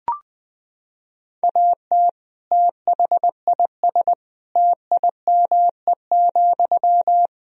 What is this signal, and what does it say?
Morse code, heard as a keyed tone, about 21 wpm, 700 Hz. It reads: AT THIS TIME,
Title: Sentences from Top 1000 Words - Review Part 2 - 20wpm